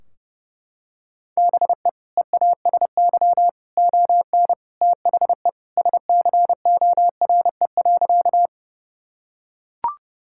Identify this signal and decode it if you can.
Morse code, heard as a keyed tone, about 30 wpm, 700 Hz. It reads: BE EASY ON THE SCORE.
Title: Sentences from Top 1600 Words 30wpm